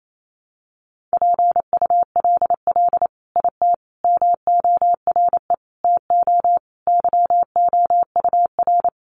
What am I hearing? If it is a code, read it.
Morse code, heard as a keyed tone, about 28 wpm, 700 Hz. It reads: PULL IT MORE TO YOUR
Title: Sentences from Top 400 Words 28wpm